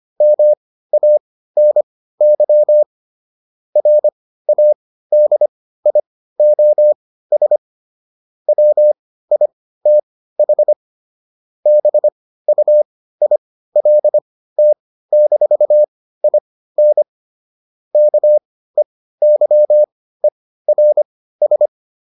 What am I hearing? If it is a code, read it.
Morse code, heard as a keyed tone, about 25 wpm, spaced as for 15 wpm, 600 Hz. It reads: MANY RADIOS WITH BUILT-IN KEYERS